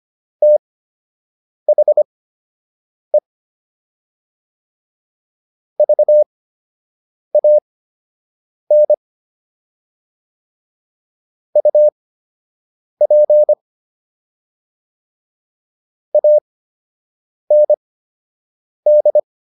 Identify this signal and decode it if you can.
Morse code, heard as a keyed tone, about 25 wpm, spaced as for 7 wpm, 600 Hz. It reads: THE VAN UP AND